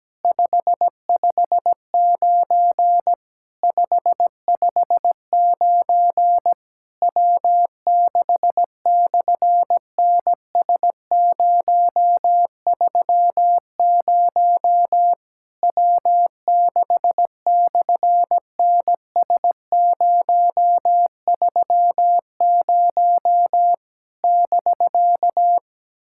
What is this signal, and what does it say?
Morse code, heard as a keyed tone, about 17 wpm, 700 Hz. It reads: 559 559 W6/NS030 W6/NS030 <BK>
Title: SOTA QSO Practice 17wpm